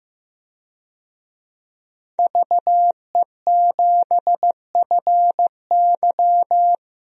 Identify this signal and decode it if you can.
Morse code, heard as a keyed tone, about 15 wpm, 700 Hz. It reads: VE7FY
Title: DXCC-100 Call Signs - 5 Characters - Repeated 3x - 15wpm